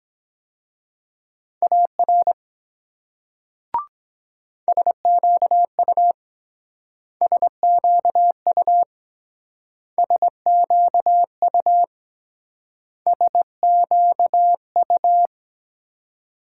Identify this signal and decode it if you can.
Morse code, heard as a keyed tone, about 23 wpm, 700 Hz. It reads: AR SQU SQU SQU SQU